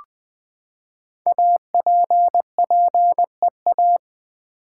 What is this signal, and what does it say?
Morse code, heard as a keyed tone, about 20 wpm, 700 Hz. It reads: APPEA
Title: Build A Word - 5 Letters At A Time - 10-Letter Words - 20wpm